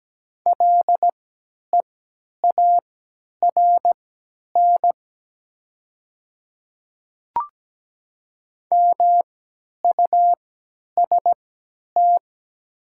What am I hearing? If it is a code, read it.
Morse code, heard as a keyed tone, about 17 wpm, spaced as for 10 wpm, 700 Hz. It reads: LEARN MUST
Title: 3x Character Spacing - Top 300 Words - 17wpm